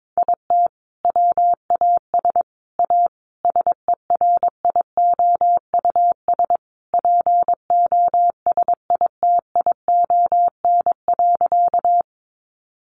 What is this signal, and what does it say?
Morse code, heard as a keyed tone, about 22 wpm, 700 Hz. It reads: IT WAS A SERIOUS POSITION.
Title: Sentences from Top 1100 Words 22wpm